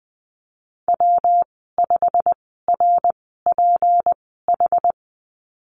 Morse code, heard as a keyed tone, about 20 wpm, spaced as for 15 wpm, 700 Hz. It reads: W5RPH
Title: 167 Lesson 27g - Number 7 - Call Signs 20wpm